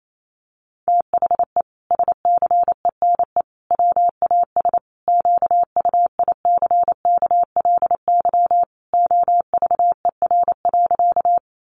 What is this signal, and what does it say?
Morse code, heard as a keyed tone, about 28 wpm, 700 Hz. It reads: THE SCENE WAS QUICKLY OVER.
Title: Sentences from Top 1100 Words 28wpm